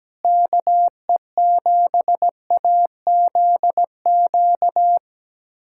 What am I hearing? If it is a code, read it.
Morse code, heard as a keyed tone, about 17 wpm, 700 Hz. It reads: KE7AZQ
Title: USA Callsign DE Callsign - 6 Characters - 17wpm